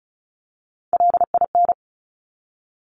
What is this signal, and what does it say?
Morse code, heard as a keyed tone, about 36 wpm, 700 Hz. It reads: LIN
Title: Top Three Letter Combinations in Words 35wpm